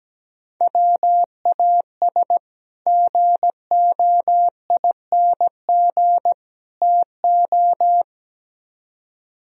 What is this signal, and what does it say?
Morse code, heard as a keyed tone, about 17 wpm, 700 Hz. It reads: WAS GOING TO
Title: Speed-Racing - Sets of 3 Words 17wpm